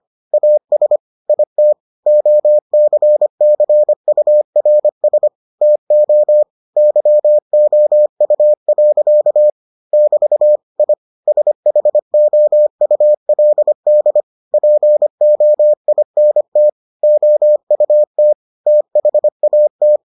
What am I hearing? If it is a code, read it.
Morse code, heard as a keyed tone, about 25 wpm, 600 Hz. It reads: AS IT OCCURS TO YOU. <BT> I SHOULD POINT OUT THAT